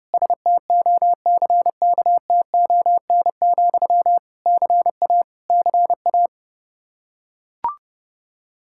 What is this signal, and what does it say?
Morse code, heard as a keyed tone, about 30 wpm, 700 Hz. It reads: STOCKTON, CA CA
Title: Practice QSO - Exchange 1 - Beginning - 30wpm